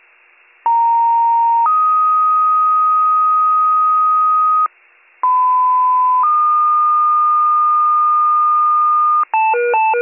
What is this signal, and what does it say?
Two-tone page, heard as a one-second tone, then a three-second tone, about 930 then 1250 Hz, about 1010 then 1220 Hz. It is heard from 0.7 to 9.2 s.